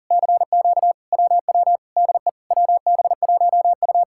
Morse code, heard as a keyed tone, about 54 wpm, spaced as for 47 wpm, 700 Hz. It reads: CQ WW DE WB1U
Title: Speed-Racing Common QSO Phrases Encore - 40wpm